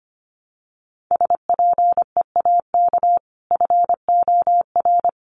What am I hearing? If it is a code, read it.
Morse code, heard as a keyed tone, about 25 wpm, 700 Hz. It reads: SPEAK FOR
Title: Sentences from Top 1100 Words 25wpm